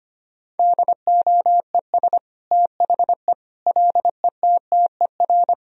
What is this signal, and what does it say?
Morse code, heard as a keyed tone, about 25 wpm, 700 Hz. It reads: DOES THE LETTER
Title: Sentences from Top 1000 Words - Review Part 2 - 25wpm